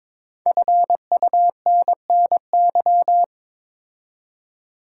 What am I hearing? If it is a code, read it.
Morse code, heard as a keyed tone, about 22 wpm, 700 Hz. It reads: FUNNY
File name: 5 Letter Words - First 3 Letters Given - 22wpm